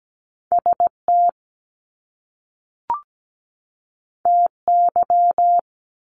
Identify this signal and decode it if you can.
Morse code, heard as a keyed tone, about 17 wpm, 700 Hz. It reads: ST TY
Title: Top Two Letter Combinations in Words 17wpm